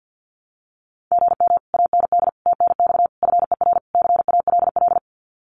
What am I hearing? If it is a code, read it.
Morse code, heard as a keyed tone, about 50 wpm, 700 Hz. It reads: GM AND TNX FER CALL
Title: Common QSO Phrases Encore 50wpm